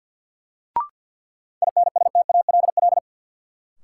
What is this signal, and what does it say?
Morse code, heard as a keyed tone, about 50 wpm, 700 Hz. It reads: INSTALL
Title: Sentences from Top 1600 Words - New Word Drill - 50wpm